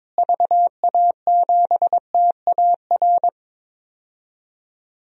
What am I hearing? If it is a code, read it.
Morse code, heard as a keyed tone, about 22 wpm, 700 Hz. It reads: VA7TAR
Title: DXCC-100 Call Signs - 6 Characters - Repeated 3x - 22wpm